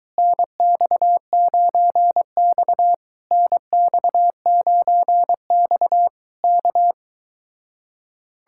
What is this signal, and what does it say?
Morse code, heard as a keyed tone, about 22 wpm, 700 Hz. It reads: NX9X NX9X K